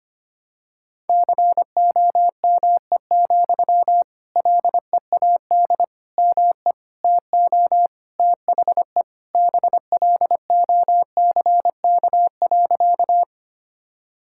Morse code, heard as a keyed tone, about 25 wpm, 700 Hz. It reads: COME, LEAD ME TO THE BLOCK.
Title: Sentences from Top 1600 Words 25wpm